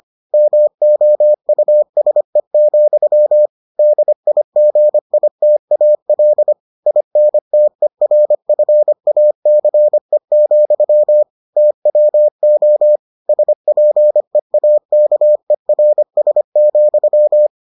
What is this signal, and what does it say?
Morse code, heard as a keyed tone, about 25 wpm, 600 Hz. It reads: MOUSE, DIGITAL INTERFACE, TWO SPEAKERS,